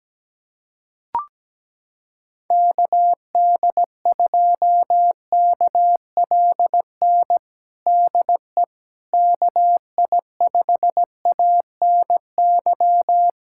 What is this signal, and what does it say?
Morse code, heard as a keyed tone, about 17 wpm, 700 Hz. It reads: KD2KLN DE KI5ANY
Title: USA Callsign DE Callsign - 6 Characters - 17wpm